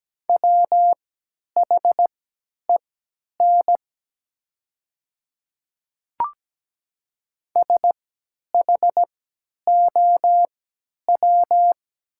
Morse code, heard as a keyed tone, about 17 wpm, spaced as for 10 wpm, 690 Hz. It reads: WHEN SHOW
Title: 3x Character Spacing - Top 500 Words - 17wpm